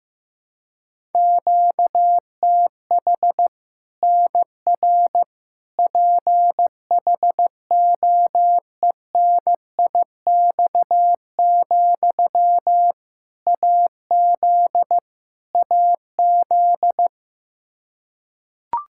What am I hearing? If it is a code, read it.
Morse code, heard as a keyed tone, about 15 wpm, 700 Hz. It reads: QTH NR PHOENIX, AZ AZ
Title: K1USN SST QSO Practice 15wpm